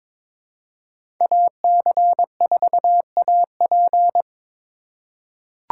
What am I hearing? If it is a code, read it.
Morse code, heard as a keyed tone, about 22 wpm, 700 Hz. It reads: AC4AP
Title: USA Call Signs - 5 Characters - 4x Character Spacing - 22wpm